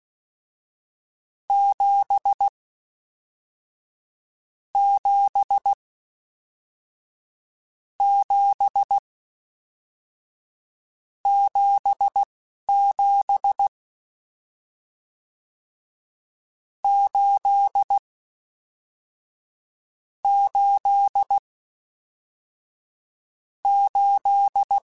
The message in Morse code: 7 7 7 7 7 8 8 8

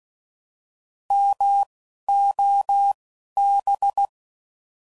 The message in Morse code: MOB

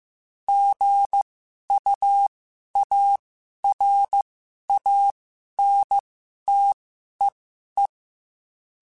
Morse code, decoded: GUARANTEE

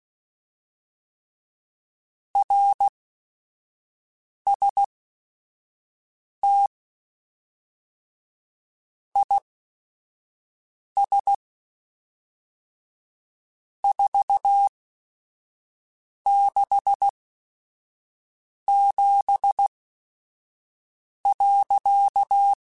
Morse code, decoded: RST IS 467.

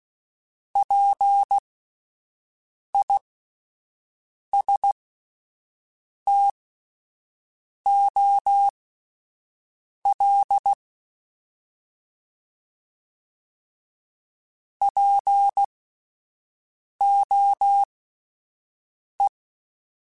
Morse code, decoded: PISTOL POE

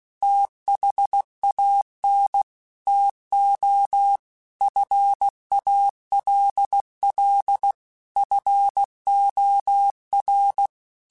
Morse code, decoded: THAN TO FALL FOR

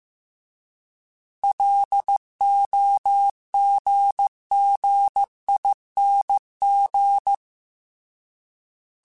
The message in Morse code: LOGGING

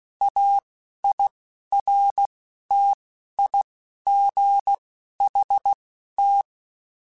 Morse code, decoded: AIRTIGHT

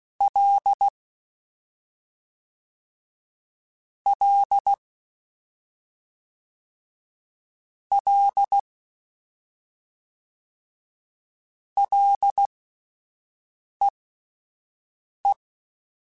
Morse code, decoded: L L L LEE

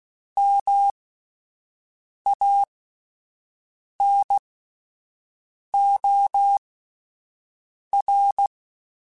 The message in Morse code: MANOR

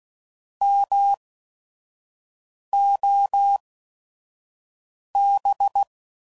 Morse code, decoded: MOB

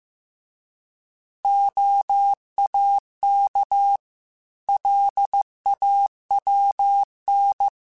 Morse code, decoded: OAK LAWN

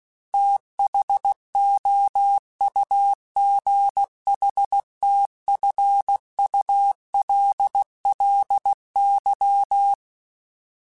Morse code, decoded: THOUGHTFULLY

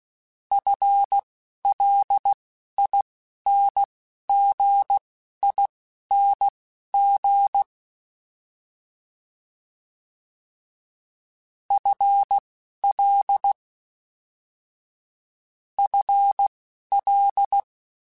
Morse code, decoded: FLINGING FL FL

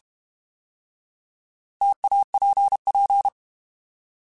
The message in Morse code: TAPP

